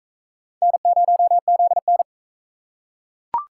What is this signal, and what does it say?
Morse code, heard as a keyed tone, about 39 wpm, 700 Hz. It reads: N0ZN